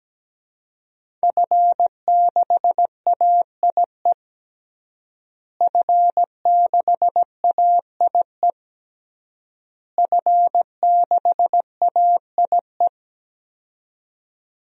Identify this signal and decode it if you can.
Morse code, heard as a keyed tone, about 17 wpm, 700 Hz. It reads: F6AIE F6AIE F6AIE
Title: DXCC-100 Call Signs - 5 Characters - Repeated 3x - 17wpm